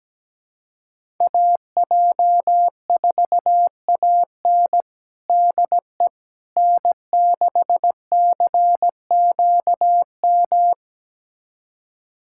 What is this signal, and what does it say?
Morse code, heard as a keyed tone, about 17 wpm, 700 Hz. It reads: AJ4AN DE N6CQM
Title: USA Callsign DE Callsign - 5 Characters - 17wpm